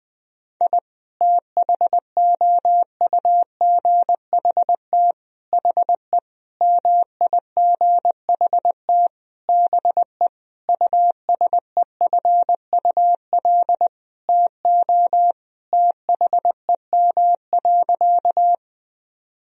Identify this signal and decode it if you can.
Morse code, heard as a keyed tone, about 20 wpm, 700 Hz. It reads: I THOUGHT HE MIGHT BE USEFUL TO THEM.
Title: Sentences from Top 1000 Words - Review Part 4 - 20wpm